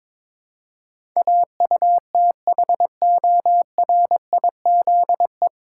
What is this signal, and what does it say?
Morse code, heard as a keyed tone, about 22 wpm, 690 Hz. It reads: AUTHORIZE
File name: 9 Letter Words - First 3 Letters Given - 22wpm